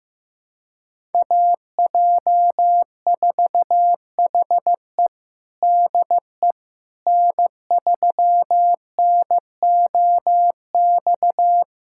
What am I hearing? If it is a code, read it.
Morse code, heard as a keyed tone, about 15 wpm, 700 Hz. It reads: AJ4HE DE N3NOX
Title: USA Callsign DE Callsign - 5 Characters - 15wpm